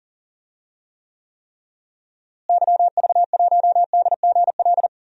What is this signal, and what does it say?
Morse code, heard as a keyed tone, about 40 wpm, 700 Hz. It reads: YV1DGL
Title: DXCC-100 Call Signs - 6 Characters - Repeated 3x - 40wpm